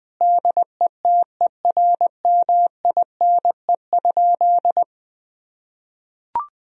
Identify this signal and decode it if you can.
Morse code, heard as a keyed tone, about 20 wpm, 700 Hz. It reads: DETERMINE?
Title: Sentences from Top 1000 Words - Review Part 4 - 20wpm